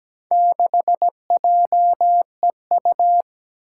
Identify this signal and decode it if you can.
Morse code, heard as a keyed tone, about 17 wpm, 700 Hz. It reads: 6JEU
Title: Four Challenging Letter-Number Combinations 17wpm